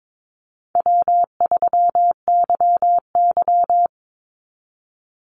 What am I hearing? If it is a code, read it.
Morse code, heard as a keyed tone, about 22 wpm, 700 Hz. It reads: W3YY
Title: Easy Call Signs Repeated 3x - 22wpm